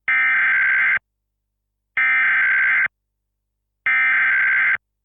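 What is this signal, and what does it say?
Emergency Alert System (SAME) tones (header bursts), heard as harsh buzzing bursts. It is heard from 0.1 to 4.8 s.
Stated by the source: Emergency Alert System tones as a soundboard sound effect.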